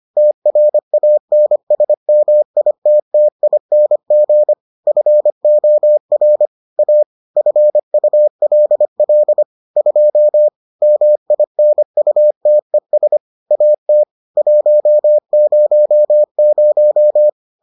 Morse code, heard as a keyed tone, about 25 wpm, 600 Hz. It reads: TRANSMITTING FOR A FULL 2 MINUTES AT 100